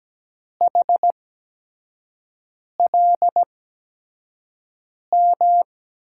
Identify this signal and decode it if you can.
Morse code, heard as a keyed tone, about 17 wpm, spaced as for 5 wpm, 700 Hz. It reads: HLM
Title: USA Call Signs - 8x Character Spacing - 17wpm